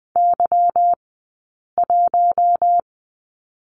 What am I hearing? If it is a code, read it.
Morse code, heard as a keyed tone, about 20 wpm, spaced as for 8 wpm, 700 Hz. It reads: Y1
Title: 171 Lesson 28d - Number 9 - Two Characters 20wpm